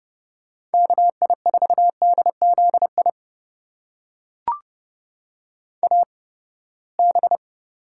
Morse code, heard as a keyed tone, about 30 wpm, spaced as for 21 wpm, 700 Hz. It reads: KI4DZI AB
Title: USA Call Signs - 8x Character Spacing - 30wpm